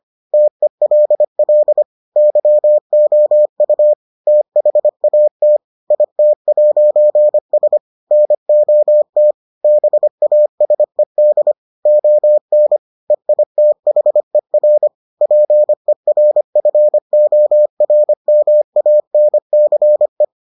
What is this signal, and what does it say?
Morse code, heard as a keyed tone, about 25 wpm, 600 Hz. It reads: TELL YOU THAT IT'S NOT BASED ON EITHER PERFORMANCE